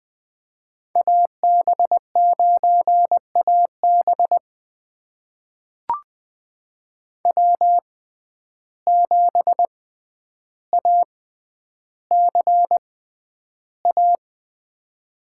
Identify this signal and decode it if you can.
Morse code, heard as a keyed tone, about 20 wpm, spaced as for 10 wpm, 700 Hz. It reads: AB9AB W7ACA
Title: USA Call Signs - 5 Characters - 6x Character Spacing - 20wpm